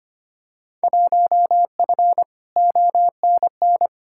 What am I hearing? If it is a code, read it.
Morse code, heard as a keyed tone, about 25 wpm, 700 Hz. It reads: 1F ONN
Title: Field Day Exchanges Encore - 25wpm